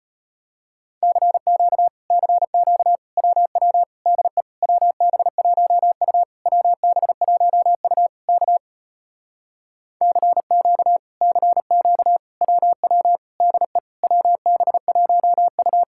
Morse code, heard as a keyed tone, about 33 wpm, 700 Hz. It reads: CQ CQ WW DE WB1U WB1U K CQ CQ WW DE WB1U